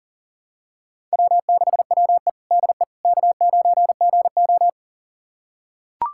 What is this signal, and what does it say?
Morse code, heard as a keyed tone, about 40 wpm, 700 Hz. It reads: W6WE DE K9GO
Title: USA Callsign DE Callsign - 4 Characters - 40wpm